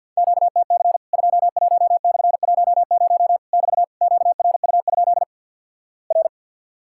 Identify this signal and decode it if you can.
Morse code, heard as a keyed tone, about 50 wpm, 700 Hz. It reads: QTC 21/10 <BT> QRU? R
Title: Worked All Europe DX Contest - QTC Exchange with Repeats - 50wpm